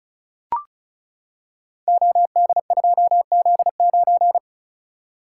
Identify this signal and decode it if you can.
Morse code, heard as a keyed tone, about 36 wpm, 700 Hz. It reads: OD2Z9
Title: Code Groups - 5 Letter:Numbers - 35wpm